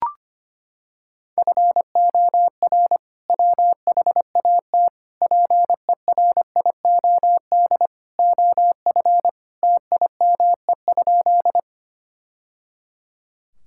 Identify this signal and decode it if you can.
Morse code, heard as a keyed tone, about 25 wpm, 700 Hz. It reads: FOR WHAT PERIOD OF TIME?
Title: Sentences from Top 1000 Words - Review Part 5 - 25wpm